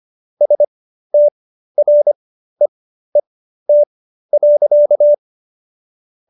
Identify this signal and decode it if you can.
Morse code, heard as a keyed tone, about 25 wpm, spaced as for 13 wpm, 590 Hz. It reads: STREET.